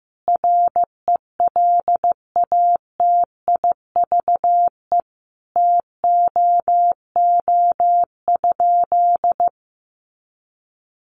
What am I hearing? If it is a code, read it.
Morse code, heard as a keyed tone, about 15 wpm, 700 Hz. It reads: RELATIVE TOO?
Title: Sentences from Top 1400 Words 15wpm